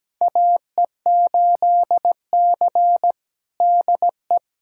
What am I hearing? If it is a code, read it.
Morse code, heard as a keyed tone, about 17 wpm, 700 Hz. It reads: AE8C DE
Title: USA Callsign DE Callsign - 4 Characters - 17wpm